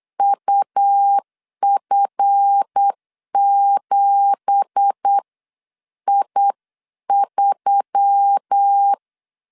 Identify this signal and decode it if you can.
Morse code, heard as a keyed tone, about 8 wpm, spaced as for 7 wpm, 800 Hz. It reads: UF7I3
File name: LEZIONI / 9 2867 8wpm 3440